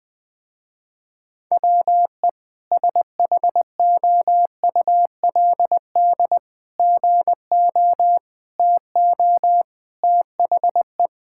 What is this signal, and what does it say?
Morse code, heard as a keyed tone, about 20 wpm, 700 Hz. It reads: WE SHOULD GO TO THE
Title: Sentences with 3-Word Phrases - Spoken Beginning - 20wpm